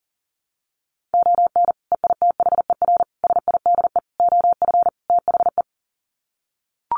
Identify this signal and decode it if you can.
Morse code, heard as a keyed tone, about 40 wpm, 700 Hz. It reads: ON EITHER SIDE OF THE
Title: Sets of 5 Words 40wpm